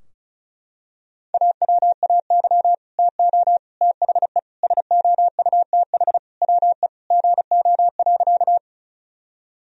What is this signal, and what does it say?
Morse code, heard as a keyed tone, about 36 wpm, 700 Hz. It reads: AWAY TO THE SOUTH WE GO.
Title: Sentences from Top 1000 Words - Review Part 2 - 35wpm